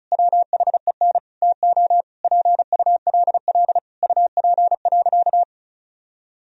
Morse code, heard as a keyed tone, about 36 wpm, 700 Hz. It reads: WHEN TO PULL UP.
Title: Sentences from Top 1000 Words - Review Part 2 - 35wpm